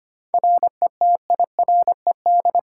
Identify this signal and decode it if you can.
Morse code, heard as a keyed tone, about 25 wpm, 700 Hz. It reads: RETIRED
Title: Practice QSO - Exchange 3 - More Info - 25wpm